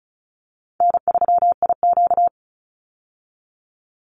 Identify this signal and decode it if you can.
Morse code, heard as a keyed tone, about 34 wpm, 700 Hz. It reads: N3IQ
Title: Easy Call Signs Repeated 3x - 35wpm